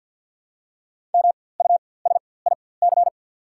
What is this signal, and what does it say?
Morse code, heard as a keyed tone, about 50 wpm, spaced as for 23 wpm, 690 Hz. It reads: MUSIC
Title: 4x Character Spacing - Top 500 Words - 50wpm